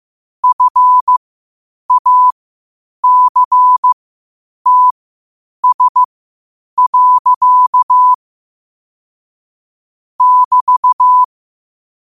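Morse code, heard as a keyed tone, about 15 wpm, spaced as for 8 wpm, 1000 Hz. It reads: FACTS. <BT>
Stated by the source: Morse code quotes at 8 words per minute from QOTD at 8 WPM.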